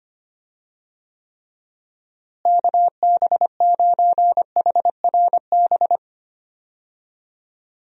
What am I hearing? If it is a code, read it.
Morse code, heard as a keyed tone, about 25 wpm, 700 Hz. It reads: KB9HRB
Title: USA Call Signs - 6 Characters - Repeated 2x - 25wpm